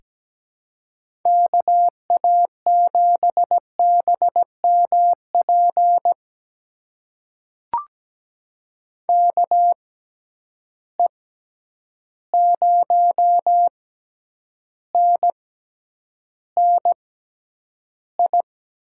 Morse code, heard as a keyed tone, about 17 wpm, spaced as for 9 wpm, 700 Hz. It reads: KA7BMP KE0NNI
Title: USA Call Signs - 6 Characters - 6x Character Spacing - 17wpm